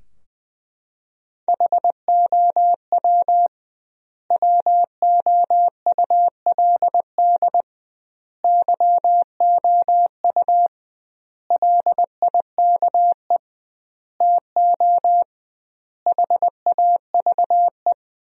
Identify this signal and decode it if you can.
Morse code, heard as a keyed tone, about 20 wpm, 700 Hz. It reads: HOW WOULD YOU LIKE TO HAVE
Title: 2x Word-Spacing - Sentences from Top 100 Words - 20wpm